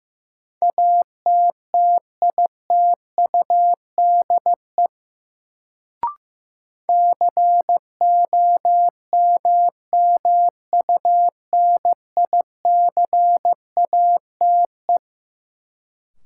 Morse code, heard as a keyed tone, about 15 wpm, 700 Hz. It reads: ATTITUDE COMMUNICATE
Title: Sentences from Top 1300 Words - New Word Drill - 15wpm